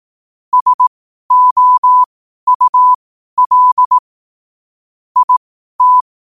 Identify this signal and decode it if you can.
Morse code, heard as a keyed tone, about 18 wpm, spaced as for 12 wpm, 1000 Hz. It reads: SOUL IT
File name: qotdfiles / 12wpm